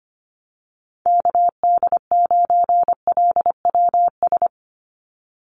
Morse code, heard as a keyed tone, about 25 wpm, 700 Hz. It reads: KD9LWS
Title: USA Call Signs 25wpm